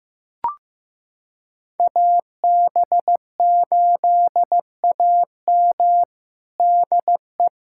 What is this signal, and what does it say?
Morse code, heard as a keyed tone, about 15 wpm, 700 Hz. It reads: AB8AM DE
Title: USA Callsign DE Callsign - 5 Characters - 15wpm